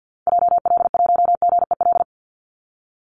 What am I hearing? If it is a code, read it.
Morse code, heard as a keyed tone, about 50 wpm, 700 Hz. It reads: JL1GEL
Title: Call Signs - 50wpm